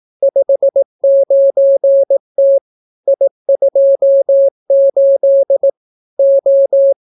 Morse code, heard as a keyed tone, about 18 wpm, 550 Hz. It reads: 59T I28 O